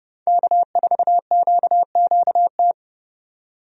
Morse code, heard as a keyed tone, about 29 wpm, 700 Hz. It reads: K4QQT
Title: USA Call Signs - 5 Characters - 30wpm